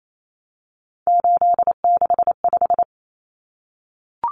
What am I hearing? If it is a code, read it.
Morse code, heard as a keyed tone, about 28 wpm, 700 Hz. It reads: 865
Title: Common Cut Numbers - 3-Digit 28wpm